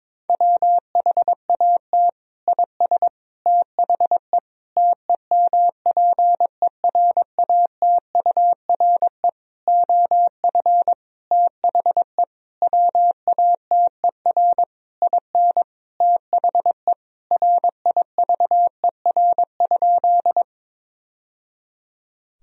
Morse code, heard as a keyed tone, about 22 wpm, 700 Hz. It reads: WHAT IS THE TEMPERATURE OF THE WATER IN THE RIVER?
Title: Sentences from Top 1000 Words - Review Part 4 - 22wpm